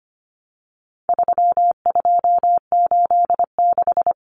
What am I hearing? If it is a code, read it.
Morse code, heard as a keyed tone, about 25 wpm, 700 Hz. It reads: 3286
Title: Common Cut Numbers - 4-Digit 25wpm